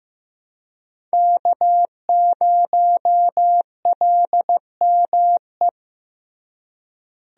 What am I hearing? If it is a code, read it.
Morse code, heard as a keyed tone, about 15 wpm, 700 Hz. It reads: K0LME
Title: USA Call Signs - 5 Characters - Repeated 2x - 15wpm